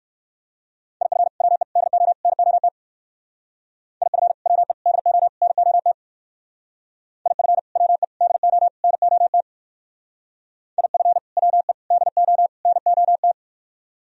Morse code, heard as a keyed tone, about 56 wpm, 700 Hz. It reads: IF WE DO NOT IF WE DO NOT IF WE DO NOT IF WE DO NOT
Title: Speed-Racing - Sets of 4 Words 45wpm